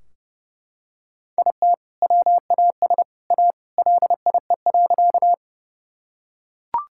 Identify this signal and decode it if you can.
Morse code, heard as a keyed tone, about 32 wpm, 700 Hz. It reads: IT WAS A LIE.
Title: Sentences from Top 1600 Words 30wpm